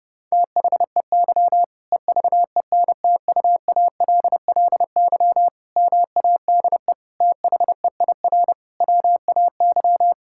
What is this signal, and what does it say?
Morse code, heard as a keyed tone, about 30 wpm, 700 Hz. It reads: THEY EVENTUALLY MADE THEIR WAY
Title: Sentences from Top 1000 Words - Review Part 6 - 30wpm